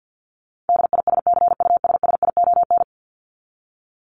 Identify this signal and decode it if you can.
Morse code, heard as a keyed tone, about 50 wpm, 700 Hz. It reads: DISCUSSION
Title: Top 1001-1250 Words 50wpm